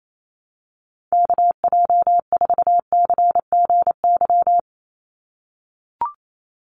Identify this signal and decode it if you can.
Morse code, heard as a keyed tone, about 28 wpm, 700 Hz. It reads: KJ4CGY
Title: USA Call Signs 28wpm